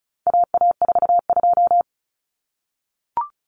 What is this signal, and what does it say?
Morse code, heard as a keyed tone, about 34 wpm, 700 Hz. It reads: AA42
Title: Common Cut Numbers - 4-Digit 35wpm